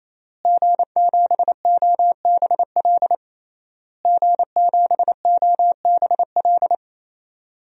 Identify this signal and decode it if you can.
Morse code, heard as a keyed tone, about 28 wpm, 700 Hz. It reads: G7OBL G7OBL
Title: DXCC-100 Call Signs - 5 Characters - Repeated 3x - 28wpm